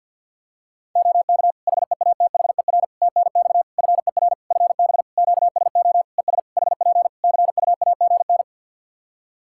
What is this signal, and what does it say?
Morse code, heard as a keyed tone, about 50 wpm, 700 Hz. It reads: OK HEATHER TNX FER FB QSO ES HP CUAGN
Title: K1USN SST QSO Practice 50wpm